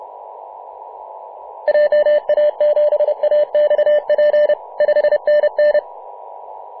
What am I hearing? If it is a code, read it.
Morse code, heard as a keyed tone, about 30 wpm, 600 Hz. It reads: JA7AXP 5NN
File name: Mar/20 2349z FW5JJ 28.011MHz CW